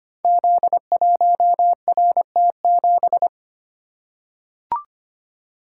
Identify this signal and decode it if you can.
Morse code, heard as a keyed tone, about 25 wpm, 700 Hz. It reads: Z1RT7
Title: Code Groups - 5 Letter:Numbers - 25wpm